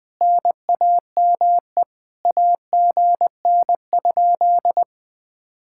Morse code, heard as a keyed tone, about 20 wpm, 700 Hz. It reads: NAME AGN?
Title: Speed-Racing Common QSO Phrases Encore - 17wpm